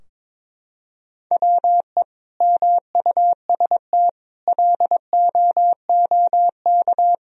Morse code, heard as a keyed tone, about 23 wpm, 700 Hz. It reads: WE MUST LOOK
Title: Sentences from Top 1000 Words - Review Part 2 - 22wpm